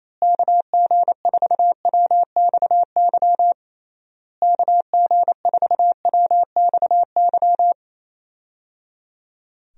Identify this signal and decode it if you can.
Morse code, heard as a keyed tone, about 28 wpm, 700 Hz. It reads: KG4WXY KG4WXY
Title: USA Call Signs - 6 Characters - Repeated 3x - 28wpm